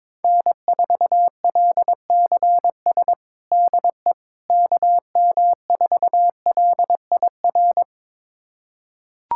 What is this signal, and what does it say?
Morse code, heard as a keyed tone, about 22 wpm, 700 Hz. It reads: N4LCS DE KM4LIR
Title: USA Callsign DE Callsign 22wpm